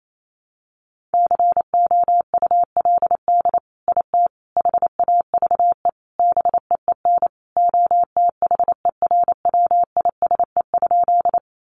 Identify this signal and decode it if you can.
Morse code, heard as a keyed tone, about 28 wpm, 700 Hz. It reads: COULD IT HAVE BEEN OTHERWISE?
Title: Sentences from Top 1200 Words 28wpm